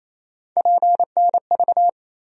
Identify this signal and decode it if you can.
Morse code, heard as a keyed tone, about 28 wpm, 700 Hz. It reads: PNV
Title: USA Call Sign Postfixes - 3 Letters 28wpm